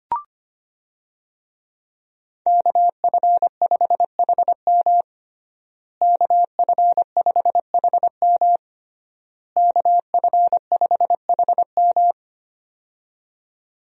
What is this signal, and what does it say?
Morse code, heard as a keyed tone, about 25 wpm, 700 Hz. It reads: KF5HM KF5HM KF5HM
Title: USA Call Signs - 5 Characters - Repeated 3x - 25wpm